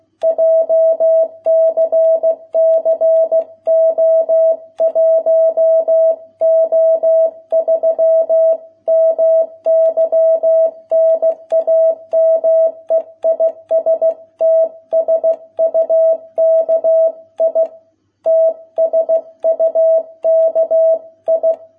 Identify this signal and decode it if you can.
Morse code, heard as a keyed tone, about 16 wpm, 650 Hz. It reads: JCC O1O3 MY NAME IS TSUKI TSUKI